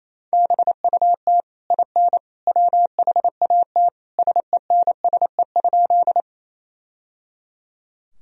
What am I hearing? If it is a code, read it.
Morse code, heard as a keyed tone, about 28 wpm, 700 Hz. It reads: BUT IN WHAT SENSE?
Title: Sentences from Top 1000 Words - Review Part 4 - 28wpm